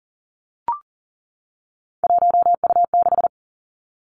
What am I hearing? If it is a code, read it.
Morse code, heard as a keyed tone, about 42 wpm, 690 Hz. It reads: 1U6
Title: Three Challenging Letter-Number Combinations 40wpm